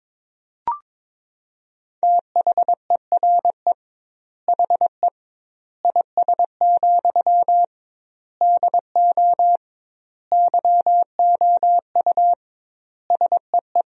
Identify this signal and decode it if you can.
Morse code, heard as a keyed tone, about 22 wpm, 700 Hz. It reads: THERE HE IS, DO YOU SEE
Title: 2x Word-Spacing - Sentences from Top 100 Words - 22wpm